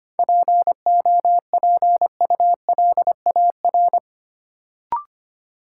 Morse code, heard as a keyed tone, about 25 wpm, 700 Hz. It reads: POPULAR
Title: Sentences from Top 1000 Words - Review Part 3 - New Word Drill - 25wpm